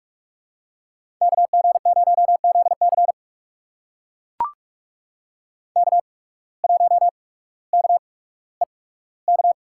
Morse code, heard as a keyed tone, about 42 wpm, spaced as for 19 wpm, 700 Hz. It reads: KG0ZC K1KEK
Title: USA Call Signs - 5 Characters - 8x Character Spacing - 45wpm